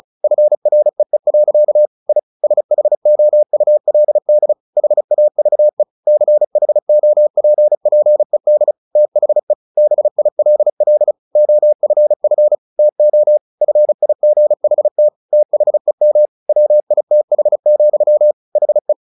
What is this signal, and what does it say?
Morse code, heard as a keyed tone, about 34 wpm, 600 Hz. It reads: FREE. I SHOULD HAVE CHOPPED THE BILL OFF TO FIGHT THEM WITH, HE
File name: the old man and the sea 2 35wpm